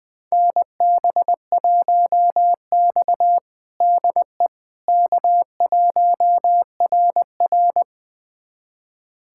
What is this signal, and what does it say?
Morse code, heard as a keyed tone, about 20 wpm, 700 Hz. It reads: NB1X DE K1RR
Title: USA Callsign DE Callsign - 4 Characters - 20wpm